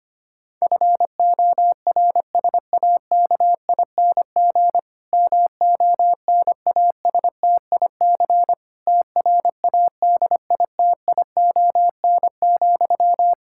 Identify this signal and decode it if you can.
Morse code, heard as a keyed tone, about 25 wpm, 700 Hz. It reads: FORSAKING MONASTIC TRADITION,
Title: Pangram Challenge v2 - 25wpm